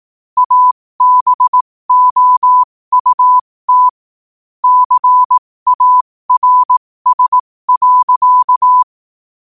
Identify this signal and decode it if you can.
Morse code, heard as a keyed tone, about 18 wpm, spaced as for 15 wpm, 1000 Hz. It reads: ABOUT CARS.
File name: qotdfiles / 15wpm